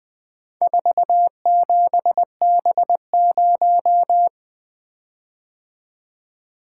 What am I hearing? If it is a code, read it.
Morse code, heard as a keyed tone, about 20 wpm, 700 Hz. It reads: 47B0
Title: Four Letter-Number Combinations - 20wpm